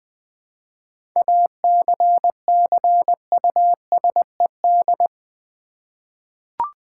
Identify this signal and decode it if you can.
Morse code, heard as a keyed tone, about 20 wpm, 700 Hz. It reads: ACCUSED
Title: 7 Letter Words - First 3 Letters Given - 20wpm